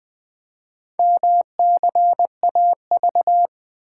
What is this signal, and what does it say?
Morse code, heard as a keyed tone, about 21 wpm, 700 Hz. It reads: MCAV
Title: Four Letter-Number Combinations - 20wpm